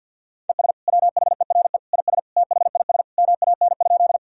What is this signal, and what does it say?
Morse code, heard as a keyed tone, about 53 wpm, 700 Hz. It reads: ES WHERE IS THIS MAN?
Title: Sentences from Top 200 Words and Abbreviations 50wpm